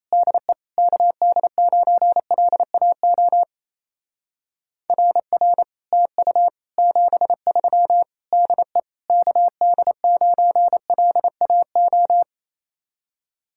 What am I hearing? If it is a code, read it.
Morse code, heard as a keyed tone, about 34 wpm, 700 Hz. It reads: DE KD9LAO RR TU 73 DE KD9LAO
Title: Speed-Racing Common QSO Phrases - 28wpm